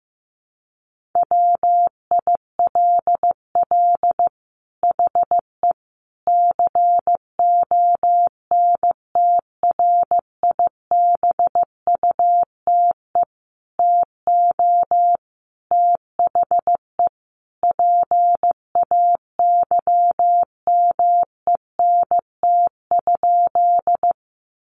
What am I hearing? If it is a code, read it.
Morse code, heard as a keyed tone, about 15 wpm, 700 Hz. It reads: WILL HE CONTRIBUTE TO THE PAYMENT?
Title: Sentences from Top 1400 Words 15wpm